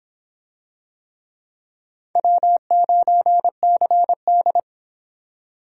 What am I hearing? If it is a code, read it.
Morse code, heard as a keyed tone, about 22 wpm, 700 Hz. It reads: W9CD